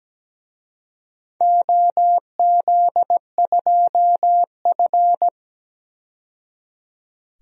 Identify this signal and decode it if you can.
Morse code, heard as a keyed tone, about 17 wpm, 700 Hz. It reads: OZ2F
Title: DXCC-100 Call Signs - 4 Characters - 17wpm